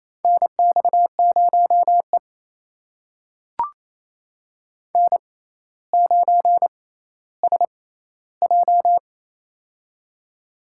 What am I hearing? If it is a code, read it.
Morse code, heard as a keyed tone, about 28 wpm, spaced as for 14 wpm, 700 Hz. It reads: NX0E N9SJ
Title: USA Call Signs - 6x Character Spacing - 28wpm